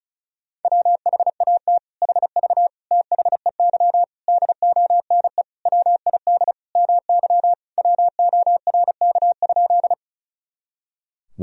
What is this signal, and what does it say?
Morse code, heard as a keyed tone, about 34 wpm, 700 Hz. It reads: WHAT HV THEY DONE WID MY WORK?
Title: Sentences from Top 500 Words and Abbreviations 35wpm